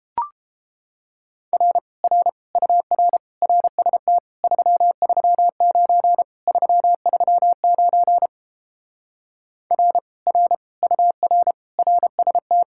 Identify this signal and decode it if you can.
Morse code, heard as a keyed tone, about 31 wpm, 700 Hz. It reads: R R UR RST 339 339 R R UR RST